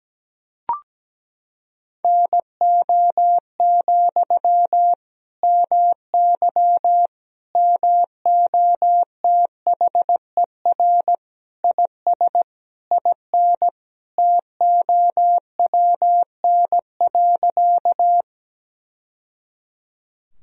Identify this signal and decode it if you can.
Morse code, heard as a keyed tone, about 17 wpm, 700 Hz. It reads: NO, MY MOTHER IS IN TOWN.
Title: Sentences from Top 1000 Words - Review Part 2 - 17wpm